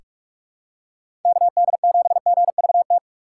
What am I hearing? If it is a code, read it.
Morse code, heard as a keyed tone, about 46 wpm, 700 Hz. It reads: KD7GVT
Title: USA Call Signs - 4x Character Spacing - 45wpm